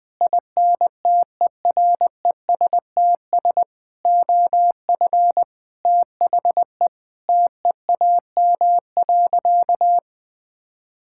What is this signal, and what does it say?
Morse code, heard as a keyed tone, about 21 wpm, 700 Hz. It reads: INTERESTS OF THE TEAM.
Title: Sentences with 6-Word Phrases - 20wpm